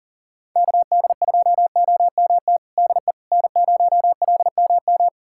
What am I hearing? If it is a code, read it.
Morse code, heard as a keyed tone, about 40 wpm, 700 Hz. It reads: KD2OMT DE N0LMM
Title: USA Callsign DE Callsign 40wpm